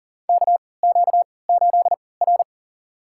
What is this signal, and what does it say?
Morse code, heard as a keyed tone, about 40 wpm, spaced as for 23 wpm, 700 Hz. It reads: KQ8R
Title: USA Call Signs - 4 Characters - 3x Character Spacing - 40wpm